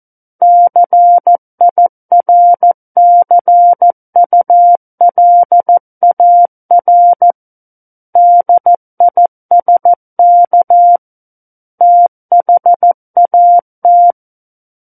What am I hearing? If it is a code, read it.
Morse code, heard as a keyed tone, about 14 wpm, 700 Hz. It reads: CIRCULAR DISK THAT
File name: Book One / 05-Chapter 5 - 14 WPM